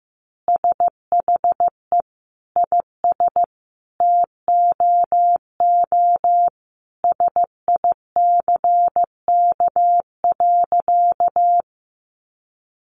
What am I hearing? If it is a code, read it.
Morse code, heard as a keyed tone, about 15 wpm, 700 Hz. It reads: SHE IS TOO SICK.
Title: Easy Phrases 15wpm